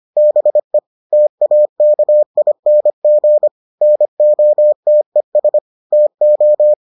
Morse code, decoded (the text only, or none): BE TAKING NOTES TO